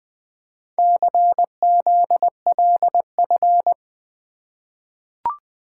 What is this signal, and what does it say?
Morse code, heard as a keyed tone, about 20 wpm, 700 Hz. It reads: CZLF
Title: Four Letter-Number Combinations - 20wpm